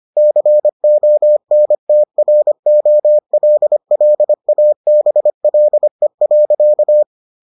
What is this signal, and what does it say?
Morse code, heard as a keyed tone, about 25 wpm, 600 Hz. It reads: CONTROLLABLE.